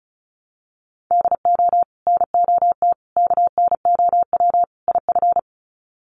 Morse code, heard as a keyed tone, about 34 wpm, 700 Hz. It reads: DO NOT KNOW IF
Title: Sets of 4 Words 35wpm